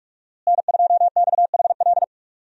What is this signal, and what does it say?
Morse code, heard as a keyed tone, about 46 wpm, 700 Hz. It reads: N2XHL
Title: USA Call Signs - 5 Characters - Repeated 2x - 45wpm